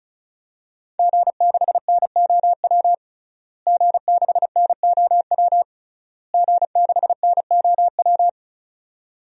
Morse code, heard as a keyed tone, about 34 wpm, 700 Hz. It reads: G6NOW G6NOW G6NOW
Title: DXCC-100 Call Signs - 5 Characters - Repeated 3x - 35wpm